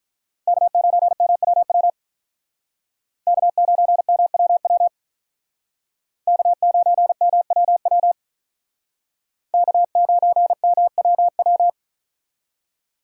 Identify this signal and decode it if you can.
Morse code, heard as a keyed tone, about 43 wpm, 700 Hz. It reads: K9MWW K9MWW K9MWW K9MWW
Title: Speed Racing - USA 1x3 Call Signs 35wpm